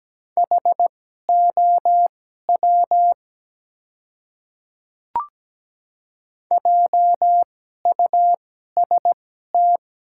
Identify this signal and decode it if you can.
Morse code, heard as a keyed tone, about 17 wpm, spaced as for 12 wpm, 700 Hz. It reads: HOW JUST
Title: 2x Character Spacing - Top 300 Words - 17wpm